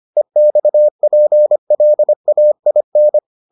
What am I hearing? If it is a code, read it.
Morse code, heard as a keyed tone, about 25 wpm, 600 Hz. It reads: EXPLAIN